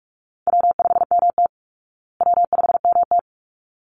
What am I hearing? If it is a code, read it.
Morse code, heard as a keyed tone, about 44 wpm, 700 Hz. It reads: W5MT W5MT
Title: Easy Call Signs Repeated 2x - 45wpm